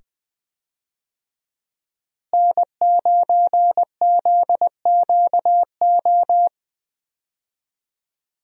Morse code, heard as a keyed tone, about 20 wpm, 700 Hz. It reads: N9ZQO
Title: USA Call Signs - 5 Characters - Repeated 2x - 20wpm